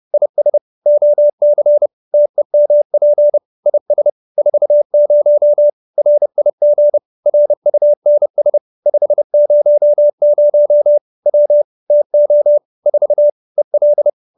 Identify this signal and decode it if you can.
Morse code, heard as a keyed tone, about 30 wpm, 590 Hz. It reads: IS OC TEMP IS 40 RIG RUNS 500 W TO 4 EL
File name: QSO308 30